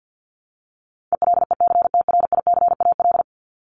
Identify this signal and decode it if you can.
Morse code, heard as a keyed tone, about 50 wpm, 700 Hz. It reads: ELECTRICAL
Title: Top 751-1000 Words 50wpm